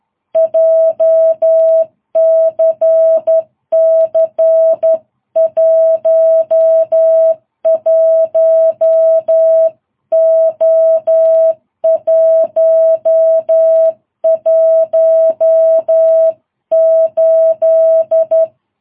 Morse code, heard as a keyed tone, about 11 wpm, 660 Hz. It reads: JCC 11O118
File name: jh8cbh / 2022cw